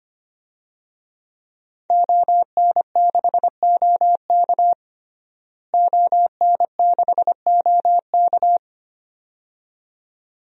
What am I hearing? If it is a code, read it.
Morse code, heard as a keyed tone, about 25 wpm, 700 Hz. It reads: ON6OK ON6OK
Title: DXCC-100 Call Signs - 5 Characters - Repeated 2x - 25wpm